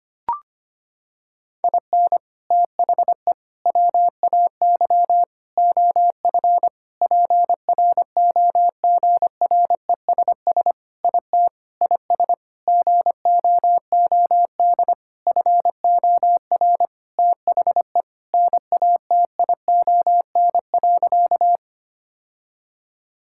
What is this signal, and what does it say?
Morse code, heard as a keyed tone, about 25 wpm, 700 Hz. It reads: IN THE WAY OF PROGRESS IT IS GOOD FOR THE NATION.
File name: practice / 25wpm / Sentences with 4-Word Phrases - 25wpm